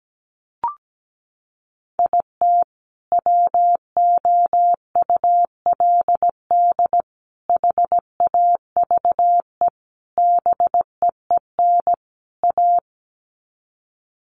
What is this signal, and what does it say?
Morse code, heard as a keyed tone, about 17 wpm, 700 Hz. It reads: IT WOULD HAVE BEEN A
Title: Sets of 5 Words Encore 17wpm